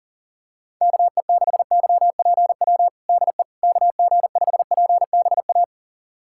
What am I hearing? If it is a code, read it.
Morse code, heard as a keyed tone, about 40 wpm, 700 Hz. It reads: KE6YPW DE KG5PBA
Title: USA Callsign DE Callsign - 6 Characters - 40wpm